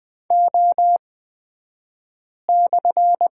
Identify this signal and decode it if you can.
Morse code, heard as a keyed tone, about 20 wpm, spaced as for 5 wpm, 690 Hz. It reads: O/
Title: 228 Lesson 37d - Letter x - Two Characters 20wpm